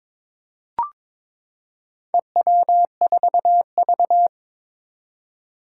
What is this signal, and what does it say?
Morse code, heard as a keyed tone, about 21 wpm, 700 Hz. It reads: EW4V
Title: Four Challenging Letter-Number Combinations 22wpm